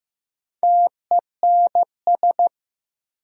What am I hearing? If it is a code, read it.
Morse code, heard as a keyed tone, about 15 wpm, 700 Hz. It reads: TENS
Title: Sets of 4 Words - New Word Drill - 15wpm